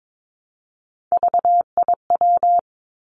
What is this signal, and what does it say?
Morse code, heard as a keyed tone, about 22 wpm, 700 Hz. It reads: VIW
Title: Three Challenging Letter-Number Combinations 22wpm